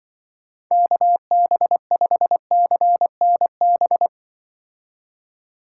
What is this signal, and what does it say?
Morse code, heard as a keyed tone, about 24 wpm, 700 Hz. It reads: KB5CNB
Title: Speed Racing - USA 2x3 Call Signs 20wpm